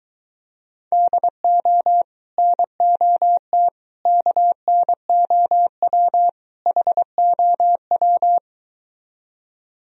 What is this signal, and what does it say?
Morse code, heard as a keyed tone, about 23 wpm, 700 Hz. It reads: DO NOT KNOW HOW